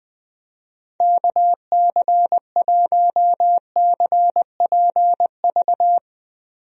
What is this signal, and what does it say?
Morse code, heard as a keyed tone, about 20 wpm, 700 Hz. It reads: KC1CPV
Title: USA Call Signs - 6 Characters - Repeated 3x - 20wpm